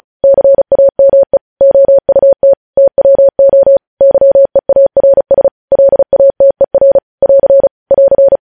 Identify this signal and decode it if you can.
Morse code, heard as a keyed tone, about 36 wpm, 550 Hz. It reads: CAME OUT TWO YEARS LATER <AR> <AR>
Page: news headlines in morse code at 35 wpm